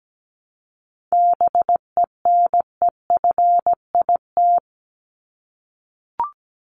Morse code, heard as a keyed tone, about 17 wpm, 700 Hz. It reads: BENEFIT
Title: Top 751-1000 Words 17wpm